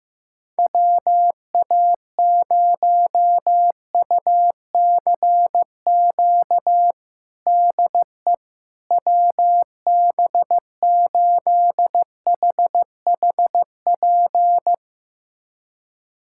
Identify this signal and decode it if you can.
Morse code, heard as a keyed tone, about 15 wpm, 700 Hz. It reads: WA0UCQ DE WB8HHP
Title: USA Callsign DE Callsign - 6 Characters - 15wpm